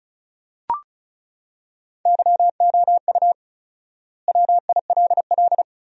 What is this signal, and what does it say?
Morse code, heard as a keyed tone, about 34 wpm, 700 Hz. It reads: YOU WILL
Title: 4x Word-Spacing - Sentences from Top 400 Words - 35wpm